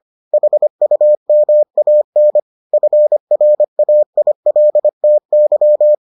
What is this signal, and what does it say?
Morse code, heard as a keyed tone, about 25 wpm, 600 Hz. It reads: HUMAN FRAILTY